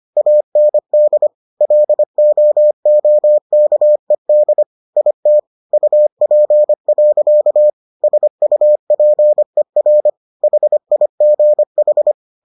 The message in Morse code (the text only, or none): AND LOOKED IT UP. SUPER HIGH